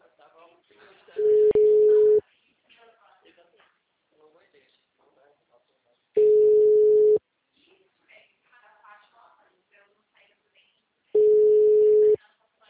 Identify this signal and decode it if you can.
Ringback tone (Europe, 425 Hz), with speech.